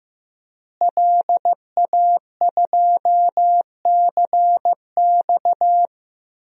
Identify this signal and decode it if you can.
Morse code, heard as a keyed tone, about 15 wpm, 700 Hz. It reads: LA2CX
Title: DXCC-100 Call Signs - 5 Characters - Repeated 2x - 15wpm